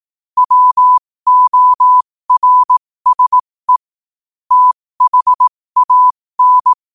Morse code, heard as a keyed tone, about 18 wpm, spaced as for 15 wpm, 1000 Hz. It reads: WORSE THAN
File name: qotdfiles / 15wpm